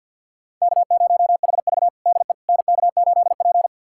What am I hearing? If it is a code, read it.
Morse code, heard as a keyed tone, about 50 wpm, 700 Hz. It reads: K0HV DE NK8P
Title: USA Callsign DE Callsign - 4 Characters - 50wpm